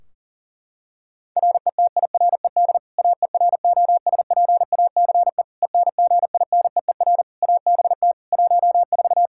Morse code, heard as a keyed tone, about 40 wpm, 700 Hz. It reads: RETIRED AEROSPACE ENGINEER ABT 14
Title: K1USN SST QSO Practice 40wpm